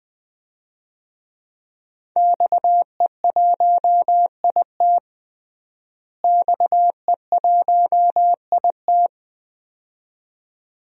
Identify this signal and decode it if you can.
Morse code, heard as a keyed tone, about 20 wpm, 700 Hz. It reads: XE1IT XE1IT
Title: DXCC-100 Call Signs - 5 Characters - Repeated 2x - 20wpm